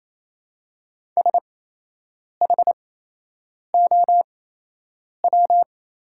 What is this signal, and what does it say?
Morse code, heard as a keyed tone, about 28 wpm, spaced as for 8 wpm, 700 Hz. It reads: SHOW
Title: 8x Character Spacing - Top 300 Words - 28wpm